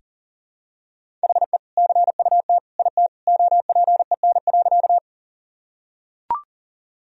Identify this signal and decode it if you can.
Morse code, heard as a keyed tone, about 40 wpm, 700 Hz. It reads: HE CUT IT OPEN.
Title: Sentences from Top 1000 Words - Review Part 1 - 40wpm